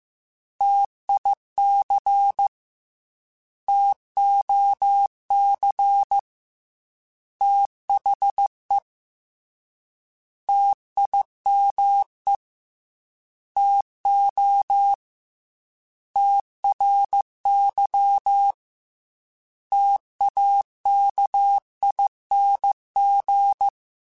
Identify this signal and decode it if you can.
Morse code, heard as a keyed tone, about 15 wpm, 800 Hz. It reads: TIC TOC THE TIME TO TRY TAKING
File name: T Runs / T Runs mp3s / T Runs @13